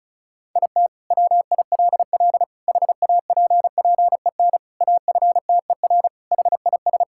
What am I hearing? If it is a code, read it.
Morse code, heard as a keyed tone, about 34 wpm, 700 Hz. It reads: IT WILL HAPPEN AFTER HIS
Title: Sentences from Top 1000 Words - Review Part 5 - 35wpm